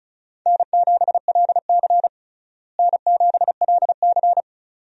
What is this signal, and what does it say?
Morse code, heard as a keyed tone, about 34 wpm, 700 Hz. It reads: N7LC N7LC
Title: USA Call Signs - 4 Characters - Repeated 2x - 35wpm